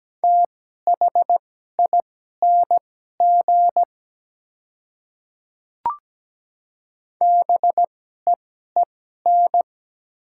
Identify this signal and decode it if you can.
Morse code, heard as a keyed tone, about 17 wpm, spaced as for 12 wpm, 700 Hz. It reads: THING BEEN
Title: 2x Character Spacing - Top 100 Words - 17wpm